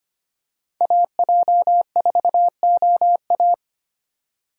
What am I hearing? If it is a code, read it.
Morse code, heard as a keyed tone, about 25 wpm, 700 Hz. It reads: AJ4OA
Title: USA Call Signs - 5 Characters - 2x Character Spacing - 25wpm